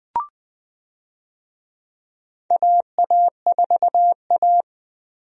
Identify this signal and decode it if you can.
Morse code, heard as a keyed tone, about 20 wpm, 700 Hz. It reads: AA4A
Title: USA Call Signs - 4 Characters - Repeated 2x - 20wpm